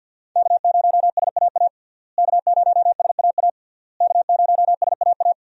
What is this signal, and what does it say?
Morse code, heard as a keyed tone, about 49 wpm, 700 Hz. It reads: K0SAA K0SAA K0SAA
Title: USA Call Signs - 5 Characters - Repeated 3x - 50wpm